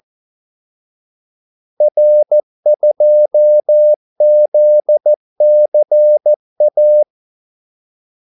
Morse code, heard as a keyed tone, about 14 wpm, 600 Hz. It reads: R2ZCA
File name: Callsigns-14wpm